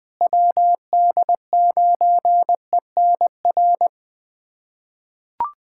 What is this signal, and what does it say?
Morse code, heard as a keyed tone, about 20 wpm, 700 Hz. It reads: WD9ENR
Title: USA Callsign DE Callsign - 6 Characters - 20wpm